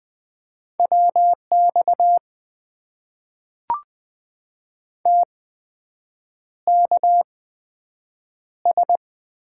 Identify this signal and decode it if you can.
Morse code, heard as a keyed tone, about 20 wpm, spaced as for 7 wpm, 700 Hz. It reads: WX TKS
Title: Top 90 QSO Elements - 8x Character Spacing - 20wpm